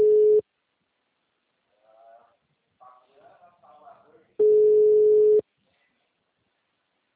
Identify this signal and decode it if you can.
Ringback tone (Europe, 425 Hz), with speech.